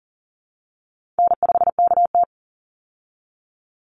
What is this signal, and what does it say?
Morse code, heard as a keyed tone, about 40 wpm, 700 Hz. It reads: N5KT
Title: Easy Call Signs Repeated 3x - 40wpm